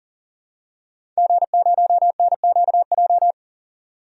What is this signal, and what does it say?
Morse code, heard as a keyed tone, about 37 wpm, 700 Hz. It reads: G0NQJ
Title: DXCC-100 Call Signs - 5 Characters - 40wpm